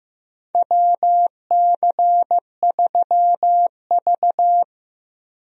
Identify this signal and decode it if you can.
Morse code, heard as a keyed tone, about 15 wpm, 700 Hz. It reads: WC3V
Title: USA Call Signs - 4 Characters - Repeated 3x - 15wpm